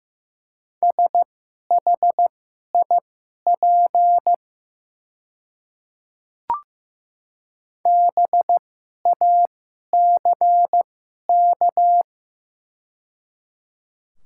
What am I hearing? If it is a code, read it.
Morse code, heard as a keyed tone, about 15 wpm, spaced as for 11 wpm, 700 Hz. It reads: SHIP BACK
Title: 2x Character Spacing - Top 500 Words - 15wpm